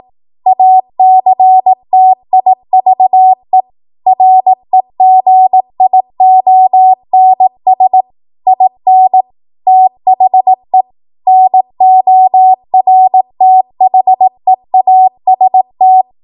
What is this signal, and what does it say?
Morse code, heard as a keyed tone, about 18 wpm, 750 Hz. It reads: ACTIVE REGIONS IN THE NORTHEAST
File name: Archive / 18 WPM / 18WPM